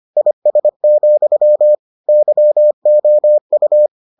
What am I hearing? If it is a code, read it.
Morse code, heard as a keyed tone, about 25 wpm, 600 Hz. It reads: IS, YOU